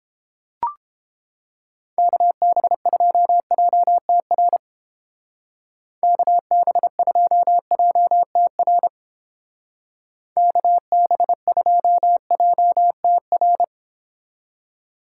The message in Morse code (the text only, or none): KB2JTR KB2JTR KB2JTR